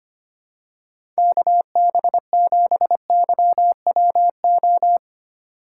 Morse code, heard as a keyed tone, about 25 wpm, 700 Hz. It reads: KB7YWO
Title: USA Call Signs - 6 Characters - 6x Character Spacing - 25wpm